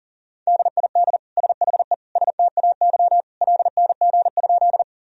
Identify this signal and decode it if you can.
Morse code, heard as a keyed tone, about 37 wpm, 700 Hz. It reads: DID SHE STAY LNG?
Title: Sentences from Top 500 Words and Abbreviations 40wpm